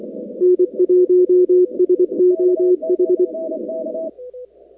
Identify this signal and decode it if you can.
Morse code, heard as a keyed tone, about 24 wpm, 360 Hz. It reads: N1SOH
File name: contests / 2019-cw-wpx-kt5j-k5pi- -2330 0-46